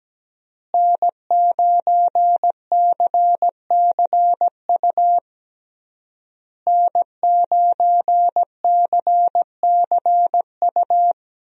Morse code, heard as a keyed tone, about 17 wpm, 700 Hz. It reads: N9CCU N9CCU
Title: USA Call Signs - 5 Characters - Repeated 3x - 17wpm